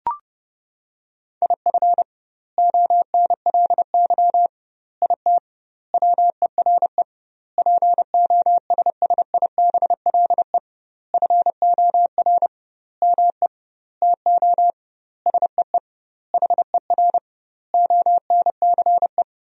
Morse code, heard as a keyed tone, about 30 wpm, 700 Hz. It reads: IF ONLY IT WERE POSSIBLE FOR ME TO SEE HER ONCE
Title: 2x Word-Spacing - Sentences from Top 500 Words - 30wpm